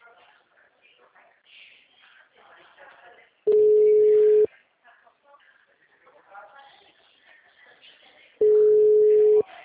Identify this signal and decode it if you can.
Ringback tone (Europe, 425 Hz), with speech.
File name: out- -2009